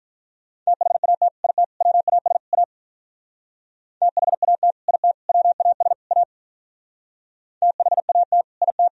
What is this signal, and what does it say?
Morse code, heard as a keyed tone, about 45 wpm, 700 Hz. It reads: THAT IT WAS A THAT IT WAS A THAT IT